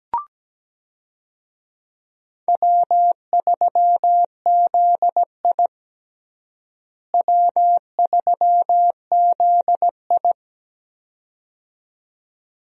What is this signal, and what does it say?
Morse code, heard as a keyed tone, about 17 wpm, 700 Hz. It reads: W3ZI W3ZI
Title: USA Call Signs - 4 Characters - Repeated 2x - 17wpm